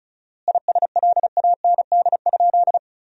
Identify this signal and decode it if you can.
Morse code, heard as a keyed tone, about 36 wpm, 700 Hz. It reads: ISLAND?
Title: 4x Word-Spacing - Sentences from Top 500 Words - 35wpm